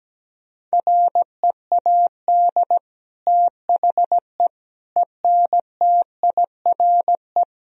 Morse code, heard as a keyed tone, about 17 wpm, 700 Hz. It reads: READ THE ENTIRE
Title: Sentences from Top 1000 Words - Review Part 4 - 17wpm